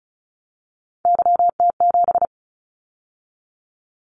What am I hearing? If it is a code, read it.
Morse code, heard as a keyed tone, about 36 wpm, 700 Hz. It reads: YT7
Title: Callsign Prefixes for Top 25 DXCC - 35wpm